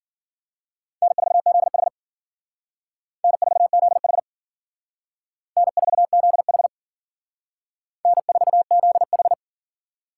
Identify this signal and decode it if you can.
Morse code, heard as a keyed tone, about 51 wpm, 700 Hz. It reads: N4ZH N4ZH N4ZH N4ZH